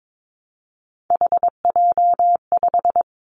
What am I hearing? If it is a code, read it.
Morse code, heard as a keyed tone, about 22 wpm, 700 Hz. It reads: HJ5
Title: Callsign Prefixes for Top 100 DXCC - 22wpm